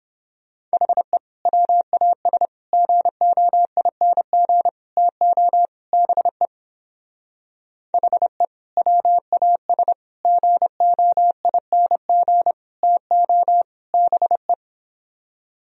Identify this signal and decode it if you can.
Morse code, heard as a keyed tone, about 29 wpm, 700 Hz. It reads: HE WAS GOING TO BE HE WAS GOING TO BE